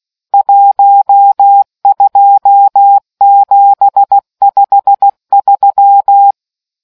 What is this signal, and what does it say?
Morse code, heard as a keyed tone, about 16 wpm, 800 Hz. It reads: 12753